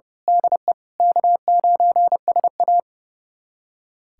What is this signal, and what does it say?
Morse code, heard as a keyed tone, about 30 wpm, 700 Hz. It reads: DE K9SA
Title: USA Callsign DE Callsign - 4 Characters - 30wpm